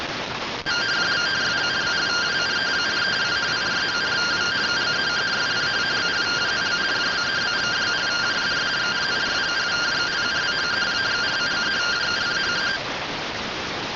Two-tone teleprinter signal, about 170 Hz shift.